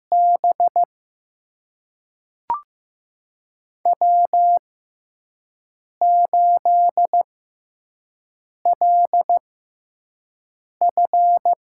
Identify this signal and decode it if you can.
Morse code, heard as a keyed tone, about 15 wpm, spaced as for 5 wpm, 700 Hz. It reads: B W8LF
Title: USA Call Signs - 6x Character Spacing - 15wpm